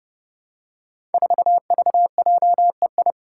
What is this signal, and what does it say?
Morse code, heard as a keyed tone, about 30 wpm, 700 Hz. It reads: 4VJEI
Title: Five Challenging Letter-Number Combinations 30wpm